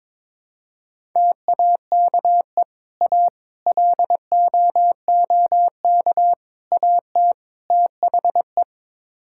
Morse code, heard as a keyed tone, about 22 wpm, 700 Hz. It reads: TAKE A LOOK AT THE
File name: practice / 22wpm / Build A Phrase - 5-Word Phrases - 22wpm